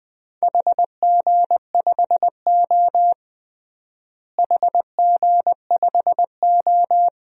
Morse code, heard as a keyed tone, about 21 wpm, 700 Hz. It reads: HG5O HG5O
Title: DXCC-100 Call Signs - 4 Characters - Repeated 2x - 20wpm